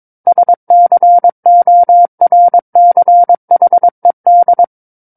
Morse code, heard as a keyed tone, about 22 wpm, 700 Hz. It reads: SCORCHED
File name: Book One / 13-Chapter 13 - 22 WPM